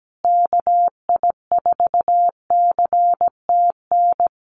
Morse code, heard as a keyed tone, about 17 wpm, 700 Hz. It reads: KI4CTN
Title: USA Call Signs Repeated 2x 17wpm